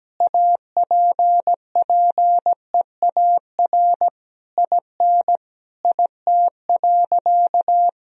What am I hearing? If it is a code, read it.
Morse code, heard as a keyed tone, about 17 wpm, 700 Hz. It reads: APPEAR IN IT.
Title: Sentences from Top 1000 Words - Review Part 2 - 17wpm